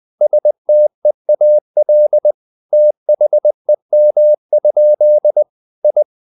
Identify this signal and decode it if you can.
Morse code, heard as a keyed tone, about 20 wpm, 600 Hz. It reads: STEAL THEM? I
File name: the old man and the sea / the old man and the sea part 1 20wpm